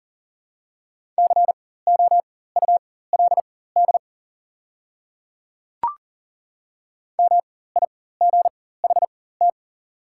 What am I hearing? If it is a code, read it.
Morse code, heard as a keyed tone, about 40 wpm, spaced as for 19 wpm, 700 Hz. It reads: COULD MIGHT
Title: 4x Character Spacing - Top 500 Words - 40wpm